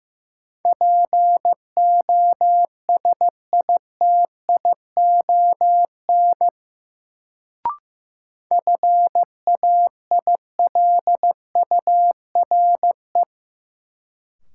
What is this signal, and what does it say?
Morse code, heard as a keyed tone, about 15 wpm, 700 Hz. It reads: POSITION FAILURE
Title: Sentences from Top 900 Words - New Word Drill - 15wpm